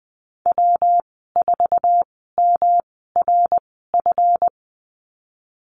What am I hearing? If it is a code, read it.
Morse code, heard as a keyed tone, about 19 wpm, spaced as for 14 wpm, 700 Hz. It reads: W4MRF
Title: 154 Lesson 25g - Letter p - Call Signs 20wpm